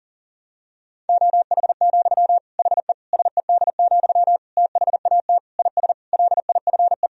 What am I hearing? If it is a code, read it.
Morse code, heard as a keyed tone, about 40 wpm, 700 Hz. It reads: OH, HE SED, THAT IS LIFE
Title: Sentences from Top 500 Words and Abbreviations 40wpm